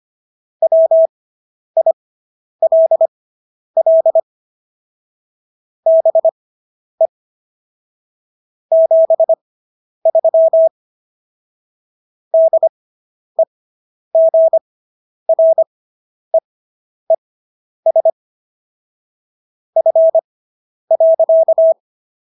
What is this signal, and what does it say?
Morse code, heard as a keyed tone, about 25 wpm, spaced as for 10 wpm, 650 Hz. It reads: WILL BE 73 DEGREES F.